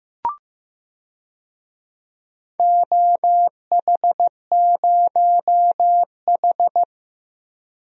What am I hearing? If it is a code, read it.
Morse code, heard as a keyed tone, about 15 wpm, 700 Hz. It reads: OH0H
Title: DXCC-100 Call Signs - 4 Characters - 15wpm